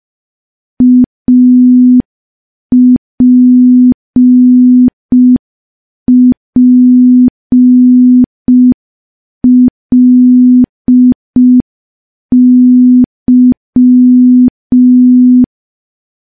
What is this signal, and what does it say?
Morse code, heard as a keyed tone, about 5 wpm, 250 Hz. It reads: APPLY
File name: morse / morse-training-05WPM